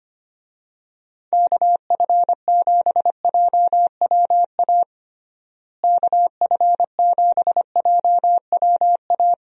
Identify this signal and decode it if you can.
Morse code, heard as a keyed tone, about 25 wpm, 700 Hz. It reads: KF7JWA KF7JWA
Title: USA Call Signs - 6 Characters - Repeated 2x - 25wpm